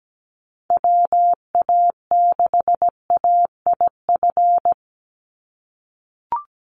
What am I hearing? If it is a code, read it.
Morse code, heard as a keyed tone, about 17 wpm, 700 Hz. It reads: WA6AIF
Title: USA Call Signs 17wpm